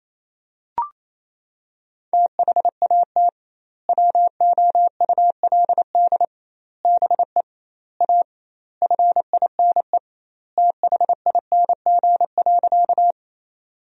Morse code, heard as a keyed tone, about 28 wpm, 700 Hz. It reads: THAT WOULD BE A FINE THING.
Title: 2x Word-Spacing - Sentences from Top 500 Words - 28wpm